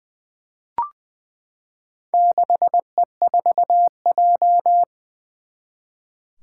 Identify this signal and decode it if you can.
Morse code, heard as a keyed tone, about 20 wpm, 700 Hz. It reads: 6E4J
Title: Four Challenging Letter-Number Combinations 20wpm